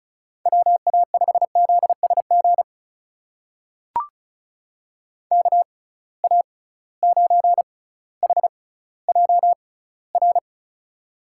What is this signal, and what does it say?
Morse code, heard as a keyed tone, about 36 wpm, spaced as for 18 wpm, 700 Hz. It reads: WA5ZSG KA9HJR
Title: USA Call Signs - 6 Characters - 6x Character Spacing - 35wpm